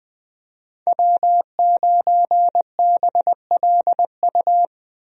Morse code, heard as a keyed tone, about 20 wpm, 700 Hz. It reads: W9BLU
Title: USA Call Signs - 5 Characters - Repeated 2x - 20wpm